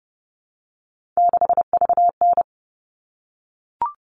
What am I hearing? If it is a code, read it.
Morse code, heard as a keyed tone, about 30 wpm, 700 Hz. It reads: 6VN
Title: All Cut Numbers - 3-Digit 30wpm